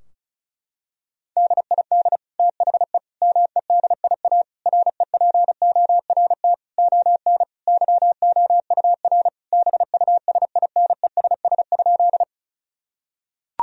Morse code, heard as a keyed tone, about 34 wpm, 700 Hz. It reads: DID THE MEDIA REPORT ON YOUR BUSINESS?
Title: Sentences from Top 1000 Words - Review Part 4 - 35wpm